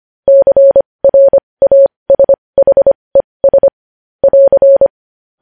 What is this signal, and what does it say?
Morse code, heard as a keyed tone, about 25 wpm, spaced as for 20 wpm, 550 Hz. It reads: CRASHES <AR>
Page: News headlines in Morse code at 20 wpm